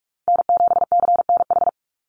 Morse code, heard as a keyed tone, about 44 wpm, 700 Hz. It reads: N7CNH
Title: NA SOTA Chaser Callsigns 45wpm